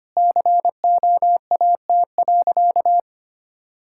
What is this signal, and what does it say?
Morse code, heard as a keyed tone, about 25 wpm, 700 Hz. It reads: COAT.
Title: Sentences from Top 1600 Words 25wpm